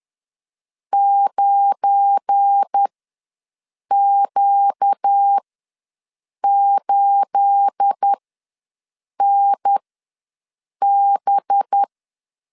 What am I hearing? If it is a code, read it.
Morse code, heard as a keyed tone, about 10 wpm, spaced as for 6 wpm, 800 Hz. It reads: 9Q8NB
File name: LEZIONI / 10 SIGNS / 10C DOT ws 10wpm 4213